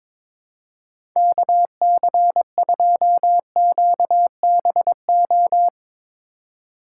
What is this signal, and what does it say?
Morse code, heard as a keyed tone, about 22 wpm, 700 Hz. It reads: KC2QBO
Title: USA Call Signs - 6 Characters - 3x Character Spacing - 22wpm